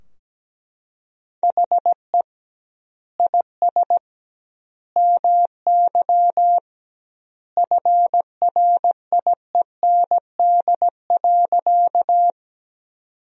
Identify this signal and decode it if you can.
Morse code, heard as a keyed tone, about 17 wpm, 700 Hz. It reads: HE IS MY FRIEND.
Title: 2x Word-Spacing - Sentences from Top 300 Words - 17wpm